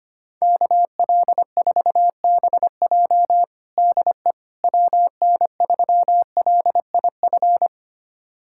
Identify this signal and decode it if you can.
Morse code, heard as a keyed tone, about 25 wpm, 700 Hz. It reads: KL4BJ DE WN3LIF
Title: USA Callsign DE Callsign 25wpm